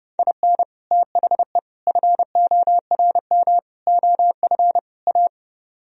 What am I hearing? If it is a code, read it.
Morse code, heard as a keyed tone, about 38 wpm, 700 Hz. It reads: IN THE FORM OF A